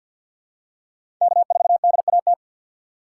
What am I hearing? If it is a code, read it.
Morse code, heard as a keyed tone, about 52 wpm, 700 Hz. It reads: K4DAT
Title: USA Call Signs - 5 Characters - 2x Character Spacing - 50wpm